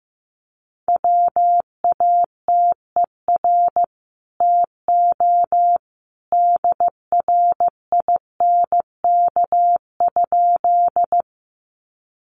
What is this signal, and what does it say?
Morse code, heard as a keyed tone, about 15 wpm, 700 Hz. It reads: WATER TO DRINK?
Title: Sentences from Top 1500 Words 15wpm